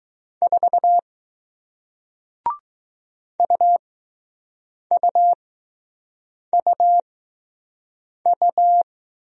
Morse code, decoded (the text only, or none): 4 UUUU